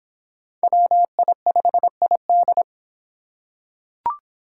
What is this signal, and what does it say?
Morse code, heard as a keyed tone, about 26 wpm, 700 Hz. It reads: WI5ID